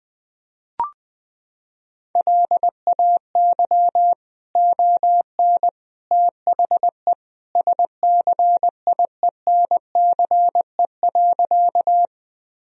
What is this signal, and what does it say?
Morse code, heard as a keyed tone, about 20 wpm, 700 Hz. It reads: LAY ON THE SCIENCE.
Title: Sentences from Top 1000 Words - Review Part 2 - 20wpm